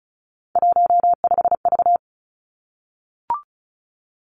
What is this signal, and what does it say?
Morse code, heard as a keyed tone, about 36 wpm, 700 Hz. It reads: 15V
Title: Three Challenging Letter-Number Combinations 35wpm